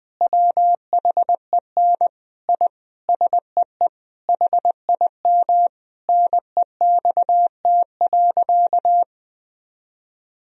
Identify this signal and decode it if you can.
Morse code, heard as a keyed tone, about 20 wpm, 700 Hz. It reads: WHEN I SEE HIM NEXT.
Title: Sentences from Top 1000 Words - Review Part 2 - 20wpm